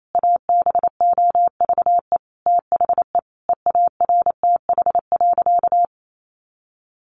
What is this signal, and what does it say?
Morse code, heard as a keyed tone, about 28 wpm, 700 Hz. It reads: ABOVE THE EARTH.
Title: A Sound Of Thunder 28wpm